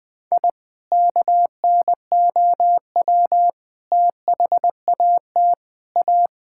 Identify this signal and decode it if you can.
Morse code, heard as a keyed tone, about 20 wpm, 700 Hz. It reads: I KNOW THAT A
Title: Sentences with 5-Word Phrases - 20wpm